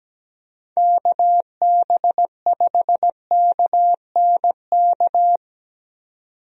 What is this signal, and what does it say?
Morse code, heard as a keyed tone, about 17 wpm, 700 Hz. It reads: KB5KNK
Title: USA Call Signs - 6 Characters - 17wpm